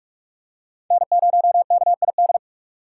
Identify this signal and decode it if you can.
Morse code, heard as a keyed tone, about 40 wpm, 700 Hz. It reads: N0KID